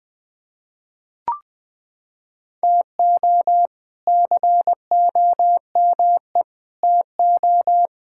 Morse code, heard as a keyed tone, about 20 wpm, 700 Hz. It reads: TO COME TO
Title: Sets of 8 Words - 20wpm